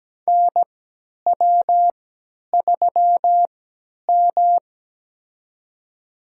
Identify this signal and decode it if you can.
Morse code, heard as a keyed tone, about 17 wpm, spaced as for 10 wpm, 700 Hz. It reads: NW3M
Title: USA Call Signs - 4 Characters - 3x Character Spacing - 17wpm